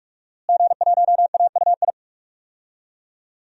Morse code, heard as a keyed tone, about 46 wpm, 700 Hz. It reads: G1AUI
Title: DXCC-100 Call Signs - 5 Characters - Repeated 2x - 45wpm